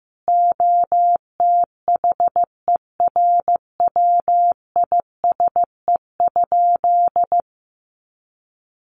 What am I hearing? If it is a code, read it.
Morse code, heard as a keyed tone, about 15 wpm, 700 Hz. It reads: OTHERWISE?
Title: Sentences from Top 1200 Words 15wpm